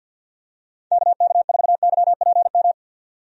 Morse code, heard as a keyed tone, about 51 wpm, 700 Hz. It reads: KK4YPM
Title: USA Call Signs - 6 Characters - 50wpm